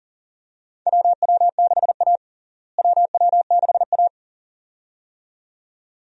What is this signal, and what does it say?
Morse code, heard as a keyed tone, about 40 wpm, 700 Hz. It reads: WW6A WW6A
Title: USA Call Signs - 4 Characters - Repeated 2x - 40wpm